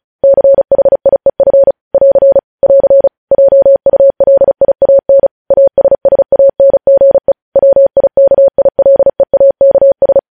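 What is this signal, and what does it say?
Morse code, heard as a keyed tone, about 36 wpm, 550 Hz. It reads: CHIEF <AR> <AR> JULIAN ASSANGE WIKILEAKS
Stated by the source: Morse code news headlines sent at 35, beginning with vvv.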